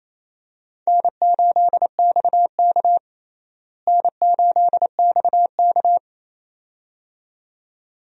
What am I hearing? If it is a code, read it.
Morse code, heard as a keyed tone, about 27 wpm, 700 Hz. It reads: N8XK N8XK
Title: USA Call Signs - 4 Characters - Repeated 2x - 28wpm